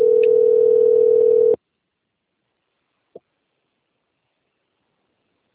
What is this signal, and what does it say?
Ringback tone (North America), 440 + 480 Hz.